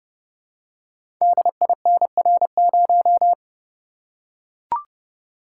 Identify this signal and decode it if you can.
Morse code, heard as a keyed tone, about 30 wpm, 700 Hz. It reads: DINR0
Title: Code Groups - 5 Letter:Numbers - 30wpm